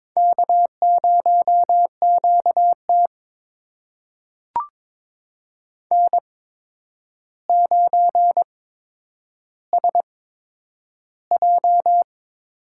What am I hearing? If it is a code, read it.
Morse code, heard as a keyed tone, about 22 wpm, spaced as for 9 wpm, 700 Hz. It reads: K0QT N9SJ
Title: USA Call Signs - 4 Characters - 8x Character Spacing - 22wpm